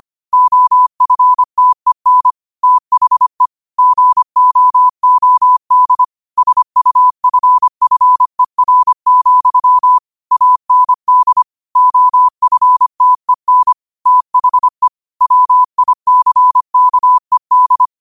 Morse code, heard as a keyed tone, about 25 wpm, 1000 Hz. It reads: OFTEN THE GOOD SUFFER, AND OFTEN THE WICKED
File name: qotdfiles / 25wpm